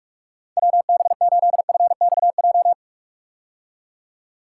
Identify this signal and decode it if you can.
Morse code, heard as a keyed tone, about 46 wpm, 700 Hz. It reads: WB8FXJ
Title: K1USN SST QSO Practice - 45wpm